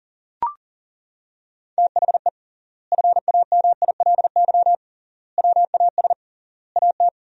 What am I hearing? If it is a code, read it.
Morse code, heard as a keyed tone, about 40 wpm, 700 Hz. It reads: THE FAMILY WAS AT
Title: 3x Word-Spacing - Sentences from Top 400 Words - 40wpm